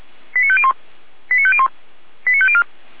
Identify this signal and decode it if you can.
Five-tone selective call (ZVEI), heard as quick runs of five tones, tones of about 70 ms. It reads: 89501 89501 89504